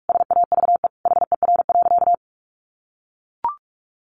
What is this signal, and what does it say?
Morse code, heard as a keyed tone, about 44 wpm, 700 Hz. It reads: SAVE HER.
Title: Sentences from Top 1100 Words 45wpm